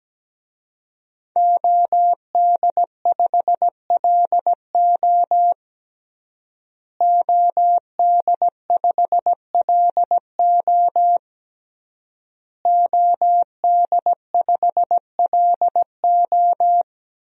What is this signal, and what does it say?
Morse code, heard as a keyed tone, about 17 wpm, 700 Hz. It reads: OD5LO OD5LO OD5LO
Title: DXCC-100 Call Signs - 5 Characters - Repeated 3x - 17wpm